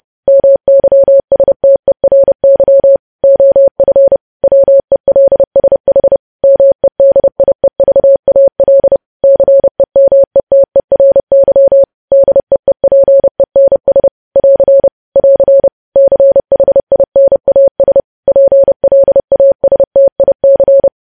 Morse code, heard as a keyed tone, about 30 wpm, 550 Hz. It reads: MYSTERY OF WELSH MEDIEVAL CEMETERY DEEPENS <AR> <AR> CHINAS PLASTIC
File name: podcast / News headlines-1x-30x30